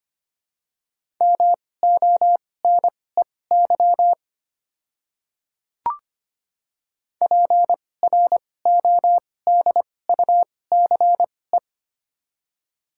Morse code, heard as a keyed tone, about 25 wpm, spaced as for 18 wpm, 700 Hz. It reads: MONEY PRODUCE
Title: 2x Character Spacing - Top 500 Words - 25wpm